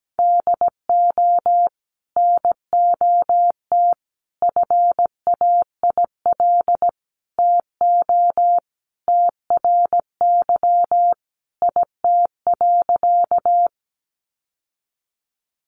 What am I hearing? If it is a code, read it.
Morse code, heard as a keyed tone, about 17 wpm, 700 Hz. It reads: DO NOT FAIL TO TRY IT.
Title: Sentences from Top 1100 Words 17wpm